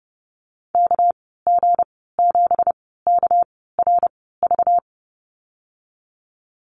Morse code, heard as a keyed tone, about 30 wpm, spaced as for 17 wpm, 700 Hz. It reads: KG7KRV
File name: Lesson 37g - Letter x - Call Signs 30wpm